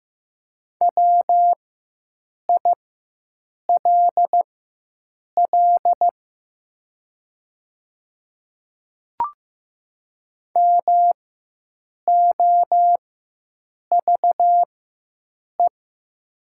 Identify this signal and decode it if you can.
Morse code, heard as a keyed tone, about 15 wpm, spaced as for 7 wpm, 700 Hz. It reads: WILL MOVE
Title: 4x Character Spacing - Top 300 Words - 15wpm